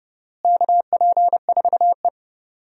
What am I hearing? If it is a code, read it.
Morse code, heard as a keyed tone, about 30 wpm, 700 Hz. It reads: KP4E
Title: DXCC-100 Call Signs - 4 Characters - Repeated 3x - 30wpm